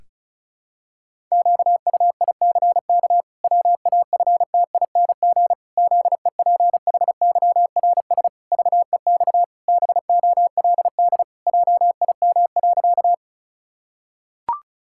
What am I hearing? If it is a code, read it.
Morse code, heard as a keyed tone, about 36 wpm, 700 Hz. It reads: QUICK WAFTING ZEPHYRS VEX BOLD JIM.
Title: Pangram Challenge v2 - 35wpm